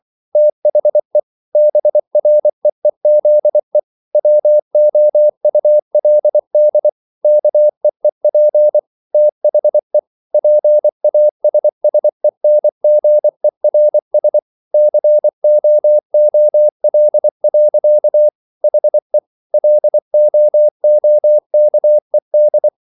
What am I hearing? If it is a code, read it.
Morse code, heard as a keyed tone, about 24 wpm, 600 Hz. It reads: THE BREEZE WOULD KEEP THE PASSENGERS COOL. HE LOOKED